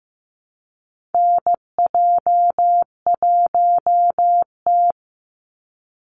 Morse code, heard as a keyed tone, about 15 wpm, 700 Hz. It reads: NJ1T
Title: Call Signs Repeated 3x - 15wpm